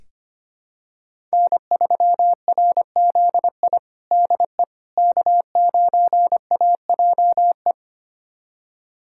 Morse code, heard as a keyed tone, about 25 wpm, 700 Hz. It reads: N3RZI DE K9AJE
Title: USA Callsign DE Callsign - 5 Characters - 25wpm